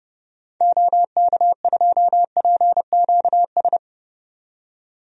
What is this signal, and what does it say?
Morse code, heard as a keyed tone, about 31 wpm, 700 Hz. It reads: OK2PQS
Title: DXCC-100 Call Signs - 6 Characters - 30wpm